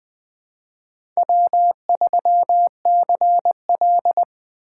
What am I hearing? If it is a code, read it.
Morse code, heard as a keyed tone, about 20 wpm, 700 Hz. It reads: W3CL
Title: USA Call Signs - 4 Characters - 3x Character Spacing - 20wpm